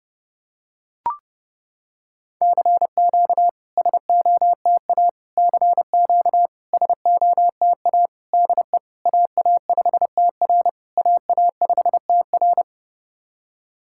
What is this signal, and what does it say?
Morse code, heard as a keyed tone, about 31 wpm, 690 Hz. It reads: CQ SOTA CQ SOTA DE AA5TR AA5TR
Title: SOTA QSO Practice 30wpm